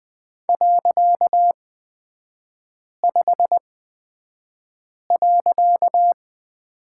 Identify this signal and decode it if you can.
Morse code, heard as a keyed tone, about 20 wpm, spaced as for 5 wpm, 700 Hz. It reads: .5.
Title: 255 Lesson 41e - Period - Three Characters 20wpm